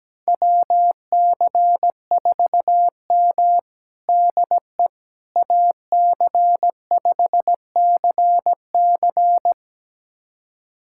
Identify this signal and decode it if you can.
Morse code, heard as a keyed tone, about 17 wpm, 700 Hz. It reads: WC4M DE AC5CC
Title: USA Callsign DE Callsign 17wpm